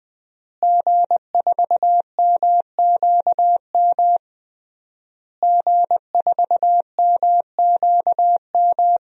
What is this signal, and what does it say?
Morse code, heard as a keyed tone, about 20 wpm, 700 Hz. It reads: G4MQM G4MQM
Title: DXCC-100 Call Signs - 5 Characters - Repeated 3x - 20wpm